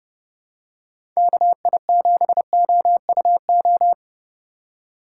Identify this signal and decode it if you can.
Morse code, heard as a keyed tone, about 30 wpm, 700 Hz. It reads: KI7OUO
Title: USA Call Signs - 6 Characters - 3x Character Spacing - 30wpm